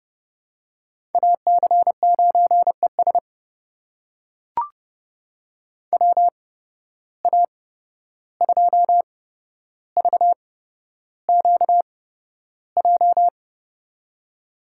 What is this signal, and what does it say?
Morse code, heard as a keyed tone, about 32 wpm, spaced as for 12 wpm, 700 Hz. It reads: AC9ES WA2VQJ
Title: USA Call Signs - 8x Character Spacing - 30wpm